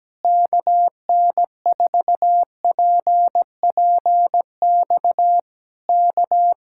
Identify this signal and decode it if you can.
Morse code, heard as a keyed tone, about 17 wpm, 700 Hz. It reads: KN4PPX K
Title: Speed-Racing Common QSO Phrases Encore - 17wpm